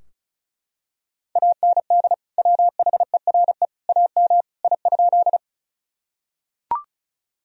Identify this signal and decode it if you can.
Morse code, heard as a keyed tone, about 34 wpm, 700 Hz. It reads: AND WHERE AM I?
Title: Sentences from Top 1000 Words - Review Part 2 - 35wpm